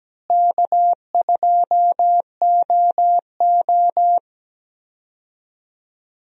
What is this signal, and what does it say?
Morse code, heard as a keyed tone, about 17 wpm, 700 Hz. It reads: K2OO
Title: USA Call Signs - 4 Characters - Repeated 2x - 17wpm